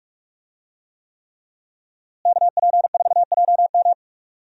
Morse code, heard as a keyed tone, about 46 wpm, 700 Hz. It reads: KP4JM
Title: DXCC-100 Call Signs - 5 Characters - Repeated 3x - 45wpm